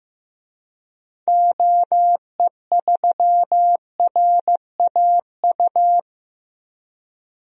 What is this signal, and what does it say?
Morse code, heard as a keyed tone, about 15 wpm, 700 Hz. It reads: OE3RAU
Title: DXCC-100 Call Signs - 6 Characters - Repeated 3x - 15wpm